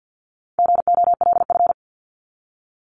Morse code, heard as a keyed tone, about 48 wpm, 700 Hz. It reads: GOLF
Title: Top 1751-2000 Words 50wpm